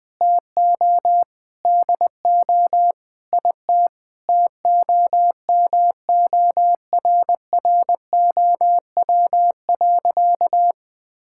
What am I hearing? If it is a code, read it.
Morse code, heard as a keyed tone, about 20 wpm, 700 Hz. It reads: TO DO IT TOMORROW.
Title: Sentences with 4-Word Phrases - 20wpm